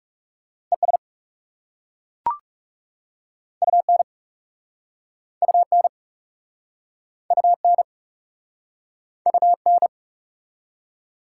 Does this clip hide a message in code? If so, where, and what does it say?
Yes: Morse code from 0.7 to 9.9 s, heard as a keyed tone, about 38 wpm, 690 Hz. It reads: ES UN UN UN UN